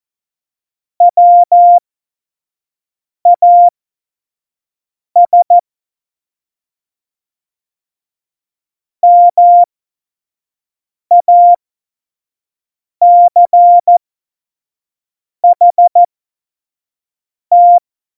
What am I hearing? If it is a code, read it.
Morse code, heard as a keyed tone, about 14 wpm, spaced as for 5 wpm, 700 Hz. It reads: WAS MACHT
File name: Morsekurs / Witze / Witz 11. September 2018, 25 F70